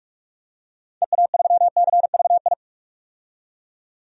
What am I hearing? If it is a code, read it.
Morse code, heard as a keyed tone, about 47 wpm, 700 Hz. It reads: EA3CVI
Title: DXCC-100 Call Signs - 6 Characters - 45wpm